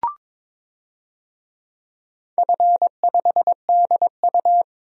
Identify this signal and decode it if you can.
Morse code, heard as a keyed tone, about 22 wpm, 700 Hz. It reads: F5DU
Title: DXCC-100 Call Signs - 4 Characters - Repeated 2x - 22wpm